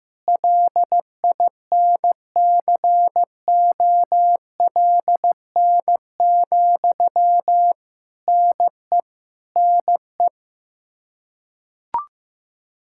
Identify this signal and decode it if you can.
Morse code, heard as a keyed tone, about 14 wpm, 700 Hz. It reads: LINCOLN, NE NE
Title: Practice QSO - Exchange 1 - Beginning - 15wpm